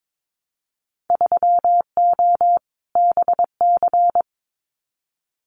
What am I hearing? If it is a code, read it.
Morse code, heard as a keyed tone, about 22 wpm, 700 Hz. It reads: 3O BC
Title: Winter Field Day Exchanges Encore 22wpm